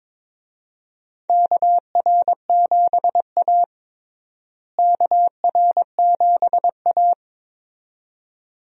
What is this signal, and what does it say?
Morse code, heard as a keyed tone, about 22 wpm, 700 Hz. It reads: KR7A KR7A
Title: USA Call Signs - 4 Characters - Repeated 2x - 22wpm